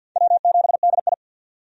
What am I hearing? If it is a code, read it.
Morse code, heard as a keyed tone, about 52 wpm, 700 Hz. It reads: W7DI
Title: USA Call Signs - 4 Characters - 50wpm